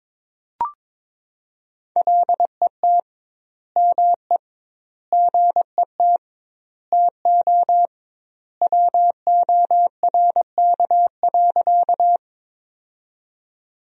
Morse code, heard as a keyed tone, about 22 wpm, 700 Hz. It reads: LET ME GET TO WORK.
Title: 2x Word-Spacing - Sentences from Top 300 Words - 22wpm